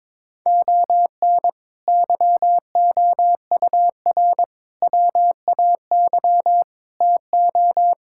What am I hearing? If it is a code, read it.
Morse code, heard as a keyed tone, about 22 wpm, 700 Hz. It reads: ON YOUR WAY TO
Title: Sentences from Top 1000 Words - Review Part 6 - 22wpm